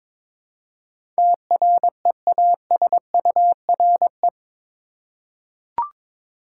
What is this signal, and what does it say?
Morse code, heard as a keyed tone, about 22 wpm, 700 Hz. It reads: TREASURE
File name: practice / 22wpm / Eight Letter Words - 22wpm